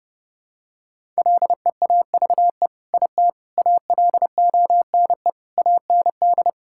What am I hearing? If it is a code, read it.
Morse code, heard as a keyed tone, about 30 wpm, 700 Hz. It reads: LEAVE IT ALONE AND
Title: Sentences from Top 1000 Words - Review Part 5 - 30wpm